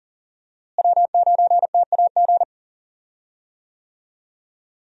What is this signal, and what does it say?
Morse code, heard as a keyed tone, about 38 wpm, 700 Hz. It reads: W9TAG
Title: USA Call Signs - 5 Characters - Repeated 3x - 40wpm